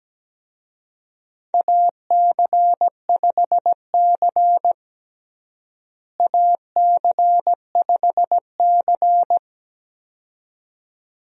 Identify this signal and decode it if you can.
Morse code, heard as a keyed tone, about 17 wpm, 700 Hz. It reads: AC5C AC5C
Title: USA Call Signs - 4 Characters - Repeated 2x - 17wpm